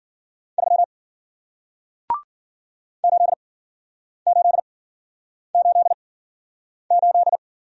Morse code, heard as a keyed tone, about 55 wpm, spaced as for 9 wpm, 700 Hz. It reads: 3 8888